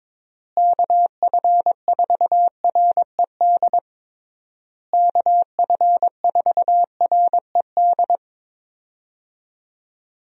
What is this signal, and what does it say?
Morse code, heard as a keyed tone, about 22 wpm, 700 Hz. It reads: KF4RED KF4RED
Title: USA Call Signs - 6 Characters - Repeated 3x - 22wpm